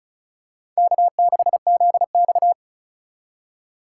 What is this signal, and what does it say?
Morse code, heard as a keyed tone, about 36 wpm, 700 Hz. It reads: K6ZX
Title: K1USN SST QSO Practice - 35wpm